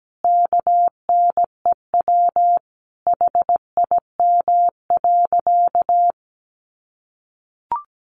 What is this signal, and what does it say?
Morse code, heard as a keyed tone, about 17 wpm, 700 Hz. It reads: KNEW HIM.
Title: Sentences from Top 1500 Words 17wpm